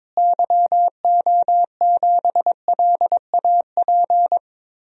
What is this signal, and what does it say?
Morse code, heard as a keyed tone, about 22 wpm, 700 Hz. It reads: YO7LAP
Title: DXCC-100 Call Signs - 6 Characters - Repeated 3x - 22wpm